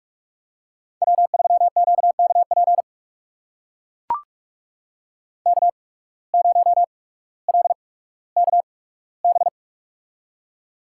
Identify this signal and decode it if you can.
Morse code, heard as a keyed tone, about 43 wpm, spaced as for 19 wpm, 700 Hz. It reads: W3QKP K0LKB
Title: USA Call Signs - 5 Characters - 8x Character Spacing - 45wpm